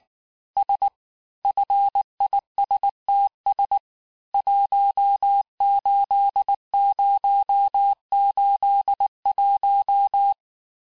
Morse code, heard as a keyed tone, about 19 wpm, 800 Hz. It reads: S FISTS 18081
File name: cours-du-lundi files / cours-1595--FIN-de-QSO-vitesse-18mpm